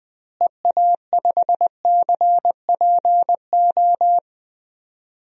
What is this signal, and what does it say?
Morse code, heard as a keyed tone, about 20 wpm, 700 Hz. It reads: EA5CPO
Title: DXCC-100 Call Signs - 6 Characters - Repeated 2x - 20wpm